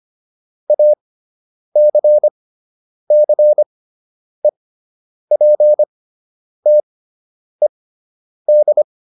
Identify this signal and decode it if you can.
Morse code, heard as a keyed tone, about 25 wpm, spaced as for 9 wpm, 600 Hz. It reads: ACCEPTED